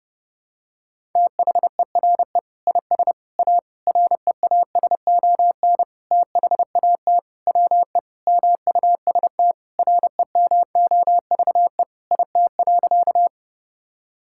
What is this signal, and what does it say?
Morse code, heard as a keyed tone, about 30 wpm, 700 Hz. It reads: THERE IS A REASON THAT WE MUST REMOVE IT.
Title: Sentences from Top 1000 Words - Review Part 5 - 30wpm